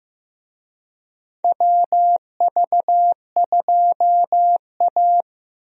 Morse code, heard as a keyed tone, about 15 wpm, 700 Hz. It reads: WV2A
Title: USA Call Signs - 4 Characters - Repeated 2x - 15wpm